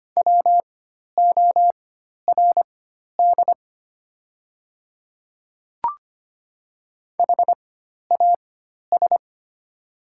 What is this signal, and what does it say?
Morse code, heard as a keyed tone, about 25 wpm, spaced as for 12 wpm, 700 Hz. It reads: WORD HAS
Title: 4x Character Spacing - Top 300 Words - 25wpm